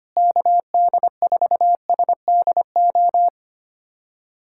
Morse code, heard as a keyed tone, about 25 wpm, 700 Hz. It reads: KD4SDO
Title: USA Callsign DE Callsign 25wpm